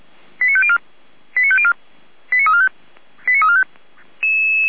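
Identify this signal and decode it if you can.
Five-tone selective call (ZVEI), heard as quick runs of five tones, tones of about 70 ms. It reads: 89504 89504 89345 89345
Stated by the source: SDR receiver recording of ZVEI five-tone selective call 89138.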